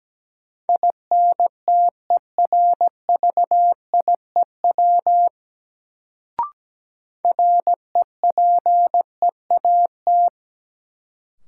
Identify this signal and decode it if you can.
Morse code, heard as a keyed tone, about 17 wpm, 700 Hz. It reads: INTERVIEW REPEAT
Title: Sentences from Top 1600 Words - New Word Drill - 17wpm